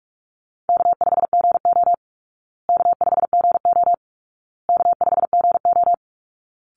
Morse code, heard as a keyed tone, about 44 wpm, 700 Hz. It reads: K5GO K5GO K5GO
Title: Call Signs Repeated 2x - 45wpm